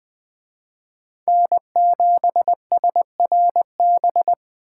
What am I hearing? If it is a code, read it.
Morse code, heard as a keyed tone, about 22 wpm, 700 Hz. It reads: N7SRB